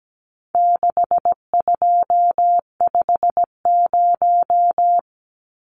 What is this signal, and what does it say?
Morse code, heard as a keyed tone, about 17 wpm, 700 Hz. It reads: 6250
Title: Numbers - 4-Digit 17wpm